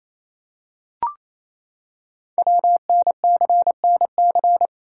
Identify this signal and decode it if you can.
Morse code, heard as a keyed tone, about 29 wpm, 700 Hz. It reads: WNCNC
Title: Code Groups - 5 Letters - 28wpm